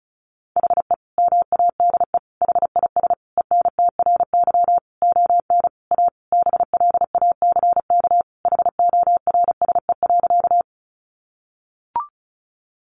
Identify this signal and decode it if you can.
Morse code, heard as a keyed tone, about 34 wpm, 700 Hz. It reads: HE MADE HIS ENTRY ON A BLACK HORSE.
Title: Sentences from Top 1500 Words 35wpm